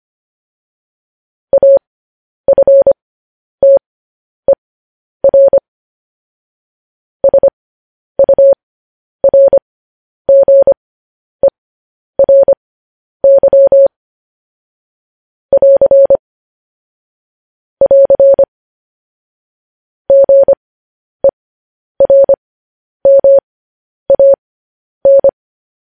Morse code, heard as a keyed tone, about 25 wpm, spaced as for 10 wpm, 550 Hz. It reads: AFTER SURGERY <AR> <AR> GERMAN
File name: podcast / News headlines-1x-25x10